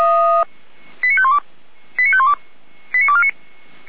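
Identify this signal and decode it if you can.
Five-tone selective call (ZVEI), heard as quick runs of five tones, tones of about 70 ms. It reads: 89512 89512 89249
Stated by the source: ZVEI five-tone selective call 89337 received via SDR, fire brigade Orlamünde (feuerwehr-orlamuende) audio archive.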